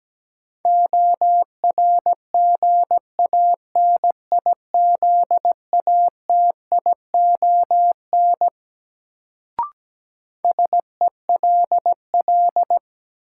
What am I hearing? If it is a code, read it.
Morse code, heard as a keyed tone, about 17 wpm, 700 Hz. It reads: ORGANIZATION SELL
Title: Sentences from Top 1100 Words - New Word Drill Alt - 17wpm